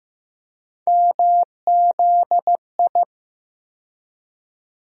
Morse code, heard as a keyed tone, about 15 wpm, 700 Hz. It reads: MZI
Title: USA Call Sign Postfixes - 3 Letters 15wpm